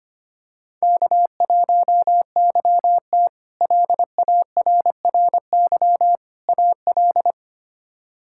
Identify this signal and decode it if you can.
Morse code, heard as a keyed tone, about 25 wpm, 700 Hz. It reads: K1YT LARRY AL
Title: K1USN SST QSO Practice - 25wpm